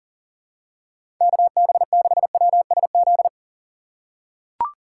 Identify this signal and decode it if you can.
Morse code, heard as a keyed tone, about 37 wpm, 700 Hz. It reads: KB6WSZ
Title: USA Call Signs - 6 Characters - 3x Character Spacing - 40wpm